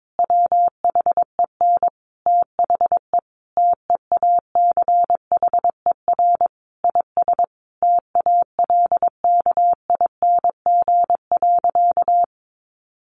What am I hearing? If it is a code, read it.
Morse code, heard as a keyed tone, about 22 wpm, 700 Hz. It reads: WHEN THE TEACHER IS TALKING.
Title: Additional Easy Phrases 22wpm